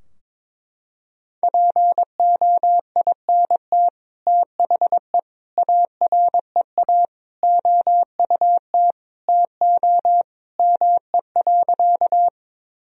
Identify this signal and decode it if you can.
Morse code, heard as a keyed tone, about 22 wpm, 700 Hz. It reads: POINT THE AREA OUT TO ME.
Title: Sentences from Top 1000 Words - Review Part 2 - 22wpm